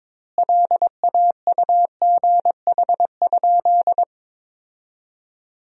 Morse code, heard as a keyed tone, about 22 wpm, 700 Hz. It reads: LAUGH?
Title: 2x Word-Spacing - Sentences from Top 500 Words - 22wpm